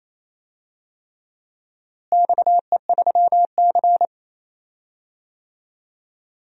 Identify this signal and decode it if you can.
Morse code, heard as a keyed tone, about 28 wpm, 700 Hz. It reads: XE3C
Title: DXCC-100 Call Signs - 4 Characters - Repeated 3x - 28wpm